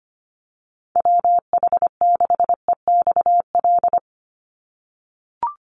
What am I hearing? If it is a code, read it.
Morse code, heard as a keyed tone, about 25 wpm, 700 Hz. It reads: WH6EXL
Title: USA Call Signs 25wpm